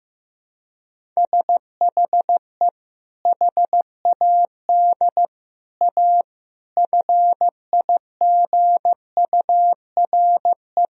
Morse code, heard as a keyed tone, about 15 wpm, 700 Hz. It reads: SHE HAD A FIGURE
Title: Sentences from Top 1000 Words - Review Part 2 - 15wpm